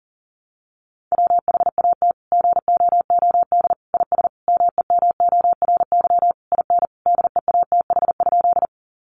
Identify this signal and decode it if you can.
Morse code, heard as a keyed tone, about 40 wpm, 700 Hz. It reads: WHAT GOOD IS MEMORY IN DEATH?
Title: Sentences from Top 1300 Words 40wpm